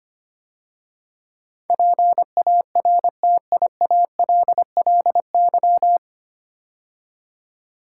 Morse code, heard as a keyed tone, about 25 wpm, 690 Hz. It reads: PARTIALLY
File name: practice / 25wpm / 9 Letter Words - First 3 Letters Given - 25wpm